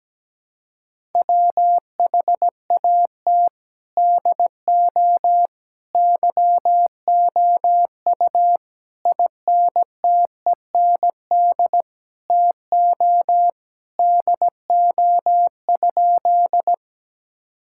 Morse code, heard as a keyed tone, about 17 wpm, 700 Hz. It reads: WHAT DO YOU INTEND TO DO?
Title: Sentences from Top 1600 Words 17wpm